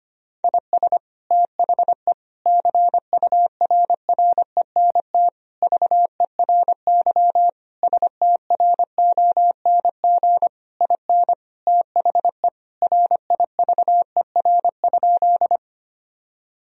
Morse code, heard as a keyed tone, about 25 wpm, 700 Hz. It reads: IS THE CURRENT VERY STRONG IN THE RIVER?
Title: Sentences from Top 1000 Words - Review Part 4 - 25wpm